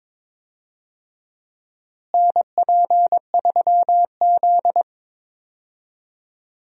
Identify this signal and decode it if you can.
Morse code, heard as a keyed tone, about 22 wpm, 700 Hz. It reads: NP3Z
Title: USA Call Signs - 4 Characters - Repeated 3x - 22wpm